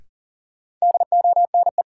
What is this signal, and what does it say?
Morse code, heard as a keyed tone, about 40 wpm, 700 Hz. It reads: DONE
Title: Sentences from Top 500 Words - New Word Drill - 40wpm